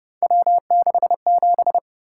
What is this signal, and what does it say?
Morse code, heard as a keyed tone, about 30 wpm, 700 Hz. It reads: W67
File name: Speed-Racing - Three Letter-Number Combinations - 25wpm